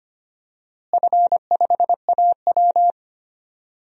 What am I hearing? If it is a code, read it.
Morse code, heard as a keyed tone, about 25 wpm, 700 Hz. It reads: F5AW
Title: DXCC-100 Call Signs - 4 Characters - Repeated 2x - 25wpm